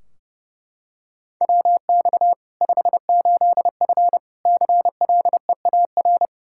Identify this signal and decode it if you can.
Morse code, heard as a keyed tone, about 30 wpm, 700 Hz. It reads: WX 58F CLEAR
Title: Practice QSO - Exchange 2 - Info - 30wpm